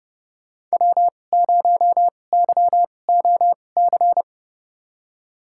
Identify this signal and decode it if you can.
Morse code, heard as a keyed tone, about 31 wpm, spaced as for 22 wpm, 700 Hz. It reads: W0YOC
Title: USA Call Signs - 5 Characters - 2x Character Spacing - 30wpm